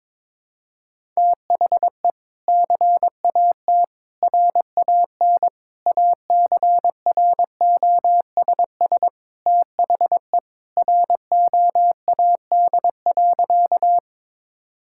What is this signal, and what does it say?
Morse code, heard as a keyed tone, about 22 wpm, 700 Hz. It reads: THE CAT RAN ACROSS THE ROAD.
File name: practice / 22wpm / Sentences with 2-Word Phrases - 22wpm